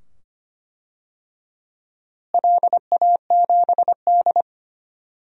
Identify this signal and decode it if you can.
Morse code, heard as a keyed tone, about 25 wpm, 700 Hz. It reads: LA7D
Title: DXCC-100 Call Signs - 4 Characters - Repeated 2x - 25wpm